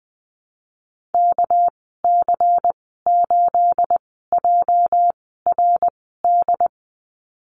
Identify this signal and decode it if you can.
Morse code, heard as a keyed tone, about 20 wpm, spaced as for 15 wpm, 700 Hz. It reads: KC8JRD
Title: 213 Lesson 34g - Number 8 - Call Signs 20wpm